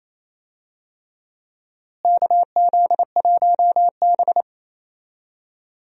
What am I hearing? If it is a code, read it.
Morse code, heard as a keyed tone, about 28 wpm, 700 Hz. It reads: KZ1B
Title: USA Call Signs - 4 Characters - Repeated 3x - 28wpm